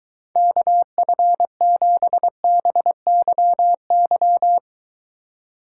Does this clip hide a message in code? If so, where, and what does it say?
Yes: Morse code from 0.4 to 4.6 s, heard as a keyed tone, about 23 wpm, 700 Hz. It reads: KF7BYY